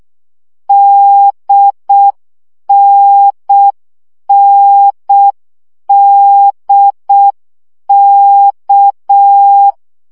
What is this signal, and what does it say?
Morse code, heard as a keyed tone, about 6 wpm, 800 Hz. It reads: DNNDK
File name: CW gyakorlas n d k 5 baud